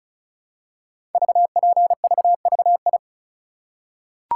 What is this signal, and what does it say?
Morse code, heard as a keyed tone, about 36 wpm, 700 Hz. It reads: VPVVI
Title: Code Groups - 5 Letters - 35wpm